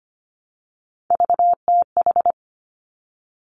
Morse code, heard as a keyed tone, about 25 wpm, 700 Hz. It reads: VTH
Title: Three Challenging Letter-Number Combinations 25wpm